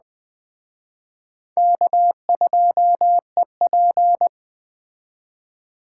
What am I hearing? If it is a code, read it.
Morse code, heard as a keyed tone, about 20 wpm, 700 Hz. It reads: K2EP
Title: Speed Racing - USA 1x2 Call Signs 17wpm